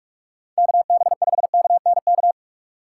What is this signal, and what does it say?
Morse code, heard as a keyed tone, about 46 wpm, 700 Hz. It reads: KB5KNK
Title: USA Call Signs - 2x Character Spacing - 45wpm